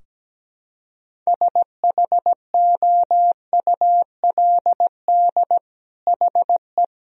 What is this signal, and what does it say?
Morse code, heard as a keyed tone, about 17 wpm, 700 Hz. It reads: SHOULD HE
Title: Sentences from Top 1000 Words - Review Part 4 - 17wpm